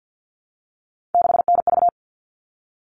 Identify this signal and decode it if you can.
Morse code, heard as a keyed tone, about 48 wpm, 700 Hz. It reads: 6NV
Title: All Cut Numbers - 3-Digit 50wpm